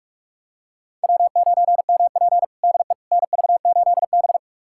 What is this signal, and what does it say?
Morse code, heard as a keyed tone, about 46 wpm, 700 Hz. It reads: W9MP DE NV8B
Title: USA Callsign DE Callsign - 4 Characters - 45wpm